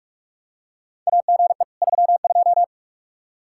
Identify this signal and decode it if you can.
Morse code, heard as a keyed tone, about 41 wpm, 700 Hz. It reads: AGE 32
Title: Practice QSO - Exchange 3 - More Info - 45wpm